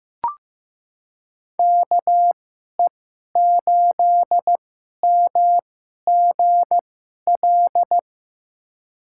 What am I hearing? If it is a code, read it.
Morse code, heard as a keyed tone, about 15 wpm, spaced as for 11 wpm, 700 Hz. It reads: KE8MGL
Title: USA Call Signs - 2x Character Spacing - 15wpm